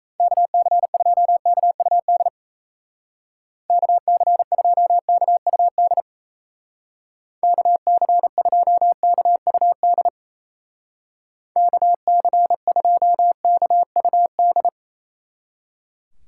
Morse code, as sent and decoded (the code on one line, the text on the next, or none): -.- -.-. ..--- -.- ..- -.. / -.- -.-. ..--- -.- ..- -.. / -.- -.-. ..--- -.- ..- -.. / -.- -.-. ..--- -.- ..- -..
KC2KUD KC2KUD KC2KUD KC2KUD